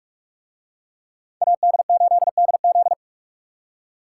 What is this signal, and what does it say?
Morse code, heard as a keyed tone, about 41 wpm, 700 Hz. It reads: AD8DZ